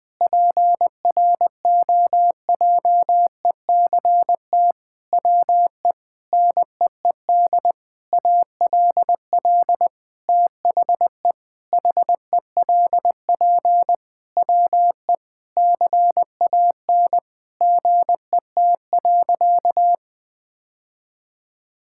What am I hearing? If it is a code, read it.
Morse code, heard as a keyed tone, about 20 wpm, 700 Hz. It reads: PROJECT WE NEED ALL THE HELP WE CAN GET.
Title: Sentences with 8-Word Phrases - Spoken Beginning - 20wpm